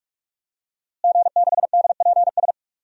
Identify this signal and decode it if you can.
Morse code, heard as a keyed tone, about 46 wpm, 700 Hz. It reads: G6DPS
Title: DXCC-100 Call Signs - 5 Characters - Repeated 2x - 45wpm